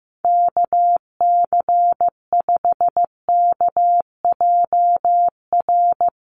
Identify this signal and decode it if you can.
Morse code, heard as a keyed tone, about 15 wpm, 700 Hz. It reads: KC5KJR
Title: USA Call Signs 15wpm